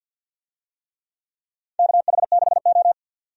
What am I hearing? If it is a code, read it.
Morse code, heard as a keyed tone, about 49 wpm, 700 Hz. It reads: KH6O
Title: DXCC-100 Call Signs - 4 Characters - 50wpm